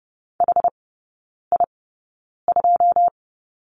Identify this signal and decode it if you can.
Morse code, heard as a keyed tone, about 30 wpm, spaced as for 9 wpm, 700 Hz. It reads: HI2
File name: 081 Lesson 14e - Number 2 - Three Characters 30wpm